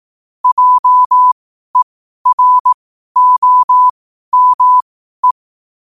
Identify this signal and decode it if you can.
Morse code, heard as a keyed tone, about 18 wpm, spaced as for 12 wpm, 1000 Hz. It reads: JEROME
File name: qotdfiles / 12wpm